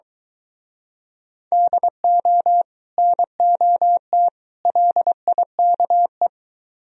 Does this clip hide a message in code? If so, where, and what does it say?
Yes: Morse code from 1.5 to 6.3 s, heard as a keyed tone, about 23 wpm, 700 Hz. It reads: DO NOT LIKE